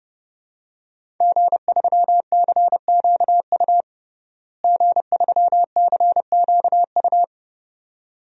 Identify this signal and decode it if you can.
Morse code, heard as a keyed tone, about 32 wpm, 700 Hz. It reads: G3CQU G3CQU
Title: DXCC-100 Call Signs - 5 Characters - Repeated 2x - 30wpm